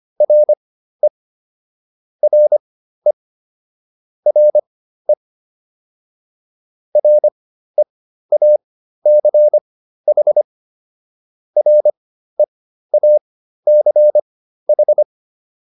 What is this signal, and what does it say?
Morse code, heard as a keyed tone, about 25 wpm, spaced as for 13 wpm, 600 Hz. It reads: RE RE RE REACH REACH